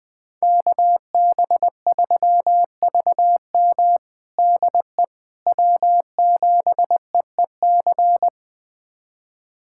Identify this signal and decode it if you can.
Morse code, heard as a keyed tone, about 20 wpm, 700 Hz. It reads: KB3VM DE W7EEC
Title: USA Callsign DE Callsign - 5 Characters - 20wpm